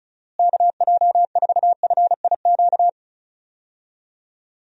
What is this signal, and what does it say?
Morse code, heard as a keyed tone, about 34 wpm, 700 Hz. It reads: KJ4FIQ
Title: USA Call Signs - 6 Characters - Repeated 3x - 35wpm